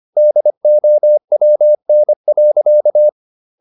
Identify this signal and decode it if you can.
Morse code, heard as a keyed tone, about 25 wpm, 600 Hz. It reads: DOWN.
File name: the old man and the sea part 1 25wpm